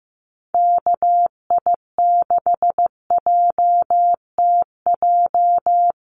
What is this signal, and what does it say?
Morse code, heard as a keyed tone, about 14 wpm, 700 Hz. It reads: KI6JTJ
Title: USA Call Signs Repeated 2x 15wpm